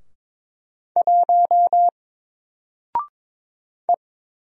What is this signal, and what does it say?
Morse code, heard as a keyed tone, about 22 wpm, 700 Hz. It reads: 1 E
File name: Single-Letter-Number-22wpm